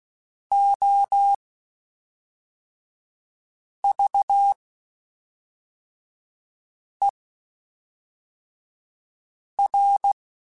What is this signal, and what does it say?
Morse code, heard as a keyed tone, about 16 wpm, spaced as for 3 wpm, 800 Hz. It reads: OVER
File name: (spaced) / QSOs Spaced-mp3s / kc7haa @5 sp